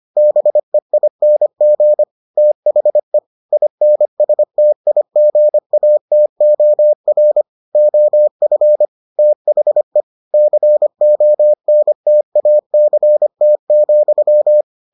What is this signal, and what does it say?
Morse code, heard as a keyed tone, about 25 wpm, 600 Hz. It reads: BEING THE INSTIGATOR OF THE CONTACT,